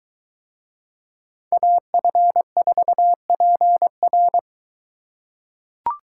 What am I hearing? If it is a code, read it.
Morse code, heard as a keyed tone, about 23 wpm, 700 Hz. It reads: AF4PR